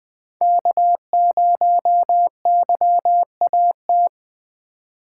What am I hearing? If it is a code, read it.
Morse code, heard as a keyed tone, about 21 wpm, 700 Hz. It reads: K0YAT
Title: USA Call Signs - 3x Character Spacing - 20wpm